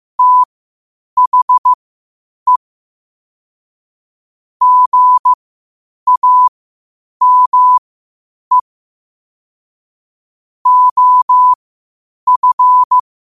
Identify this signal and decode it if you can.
Morse code, heard as a keyed tone, about 15 wpm, spaced as for 8 wpm, 1000 Hz. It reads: THE GAME OF
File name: qotdfiles / 8wpm